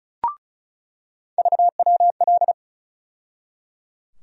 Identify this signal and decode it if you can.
Morse code, heard as a keyed tone, about 34 wpm, 700 Hz. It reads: VWL
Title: USA Call Sign Postfixes - 3 Letters 35wpm